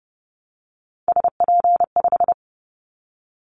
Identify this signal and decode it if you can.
Morse code, heard as a keyed tone, about 29 wpm, 700 Hz. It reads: SP5
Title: Callsign Prefixes for Top 25 DXCC - 30wpm